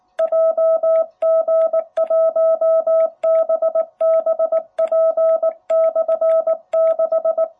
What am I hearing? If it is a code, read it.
Morse code, heard as a keyed tone, about 19 wpm, 660 Hz. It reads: JG1BBP/6
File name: jh8cbh / 2021cw